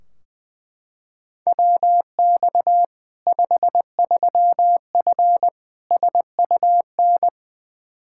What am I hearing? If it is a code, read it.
Morse code, heard as a keyed tone, about 21 wpm, 700 Hz. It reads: WX 53F SUN
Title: Practice QSO - Exchange 2 - Info - 20wpm